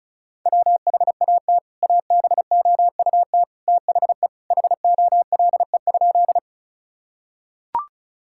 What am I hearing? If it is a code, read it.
Morse code, heard as a keyed tone, about 36 wpm, 700 Hz. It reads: WHAT ABOUT THE HOLE?
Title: Sentences from Top 1600 Words 35wpm